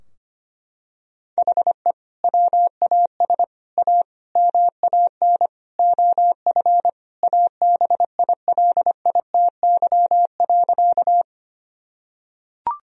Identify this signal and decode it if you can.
Morse code, heard as a keyed tone, about 25 wpm, 700 Hz. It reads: HE WAS A MAN OF ABILITY.
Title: Sentences from Top 1000 Words - Review Part 3 - 25wpm